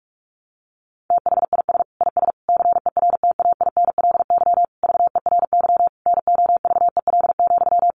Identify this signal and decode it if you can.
Morse code, heard as a keyed tone, about 44 wpm, 700 Hz. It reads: THIS IS CERTAINLY VERY NOVEL,
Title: Sentences from Top 1300 Words 45wpm